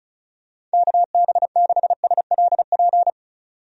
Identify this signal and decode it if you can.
Morse code, heard as a keyed tone, about 36 wpm, 700 Hz. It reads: KB6SLP
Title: USA Call Signs - 6 Characters - 35wpm